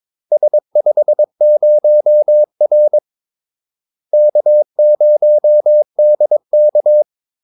Morse code, heard as a keyed tone, about 22 wpm, 600 Hz. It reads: S50R K0DK
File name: Callsigns-22wpm